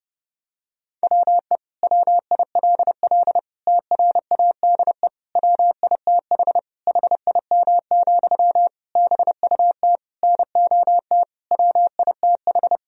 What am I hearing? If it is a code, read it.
Morse code, heard as a keyed tone, about 30 wpm, 700 Hz. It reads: WE WILL TRADE WITH HIM, BUT NOT WITH
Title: Sentences from Top 1000 Words - Review Part 4 - 30wpm